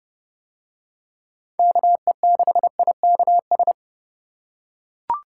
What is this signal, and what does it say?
Morse code, heard as a keyed tone, about 31 wpm, 700 Hz. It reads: KE6IKS